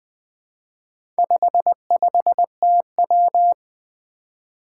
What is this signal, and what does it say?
Morse code, heard as a keyed tone, about 20 wpm, 700 Hz. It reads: 55TW
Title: Four Challenging Letter-Number Combinations 20wpm